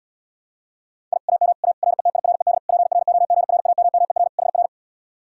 Morse code, heard as a keyed tone, about 77 wpm, 700 Hz. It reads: I AM A RETIRED PROGRAMMER HR